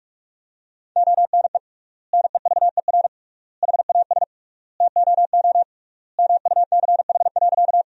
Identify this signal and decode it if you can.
Morse code, heard as a keyed tone, about 41 wpm, 700 Hz. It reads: ONE NEVER HAS TOO MUCH.
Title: 3x Word-Spacing - Sentences from Top 300 Words - 45wpm